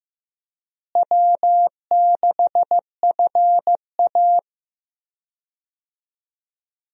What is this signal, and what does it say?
Morse code, heard as a keyed tone, about 16 wpm, 700 Hz. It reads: W6FA
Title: USA Call Signs - 4 Characters - Repeated 3x - 15wpm